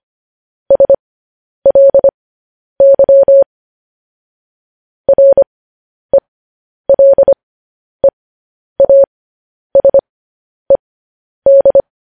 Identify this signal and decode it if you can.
Morse code, heard as a keyed tone, about 25 wpm, spaced as for 10 wpm, 550 Hz. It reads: SLY RELEASED